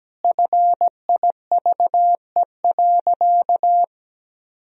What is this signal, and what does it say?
Morse code, heard as a keyed tone, about 17 wpm, 700 Hz. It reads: FIVE.
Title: Sentences from Top 1000 Words - Review Part 2 - 17wpm